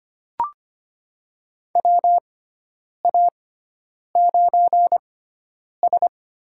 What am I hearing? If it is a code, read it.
Morse code, heard as a keyed tone, about 25 wpm, spaced as for 9 wpm, 700 Hz. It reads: WA9S
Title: USA Call Signs - 6 Characters - 6x Character Spacing - 25wpm